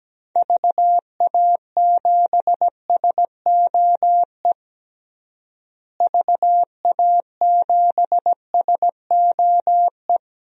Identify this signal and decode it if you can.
Morse code, heard as a keyed tone, about 17 wpm, 700 Hz. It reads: VA7SOE VA7SOE
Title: DXCC-100 Call Signs - 6 Characters - Repeated 2x - 17wpm